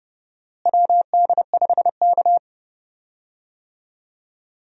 Morse code, heard as a keyed tone, about 31 wpm, 700 Hz. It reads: WD5K
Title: USA Call Signs - 4 Characters - Repeated 2x - 30wpm